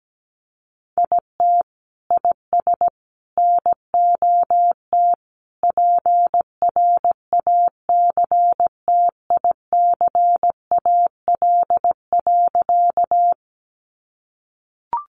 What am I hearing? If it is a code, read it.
Morse code, heard as a keyed tone, about 17 wpm, 700 Hz. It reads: IT IS NOT PRACTICAL.
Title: Sentences from Top 1400 Words 17wpm